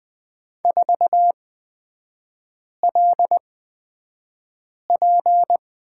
Morse code, heard as a keyed tone, about 20 wpm, spaced as for 5 wpm, 700 Hz. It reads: 4LP
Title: 224 Lesson 36e - BT - Three Characters 20wpm